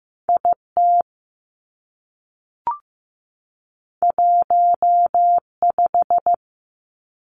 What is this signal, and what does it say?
Morse code, heard as a keyed tone, about 15 wpm, 700 Hz. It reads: I T 1 5
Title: Two Challenging Letter-Number Combinations 15wpm